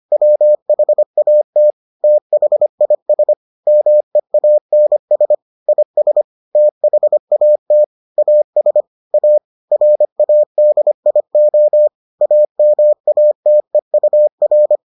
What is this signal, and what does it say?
Morse code, heard as a keyed tone, about 25 wpm, 600 Hz. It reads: WHAT THIS MEANS IS THAT AS A RADIO AMATEUR